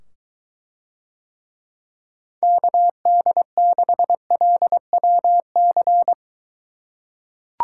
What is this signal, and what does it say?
Morse code, heard as a keyed tone, about 23 wpm, 700 Hz. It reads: KD6LWC